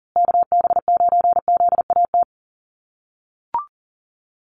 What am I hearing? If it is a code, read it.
Morse code, heard as a keyed tone, about 37 wpm, 700 Hz. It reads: KB9ZAT
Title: USA Call Signs Repeated 3x 40wpm